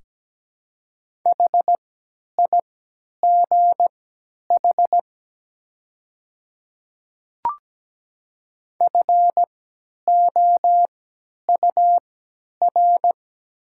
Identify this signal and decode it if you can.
Morse code, heard as a keyed tone, about 17 wpm, spaced as for 10 wpm, 700 Hz. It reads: HIGH FOUR
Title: 3x Character Spacing - Top 500 Words - 17wpm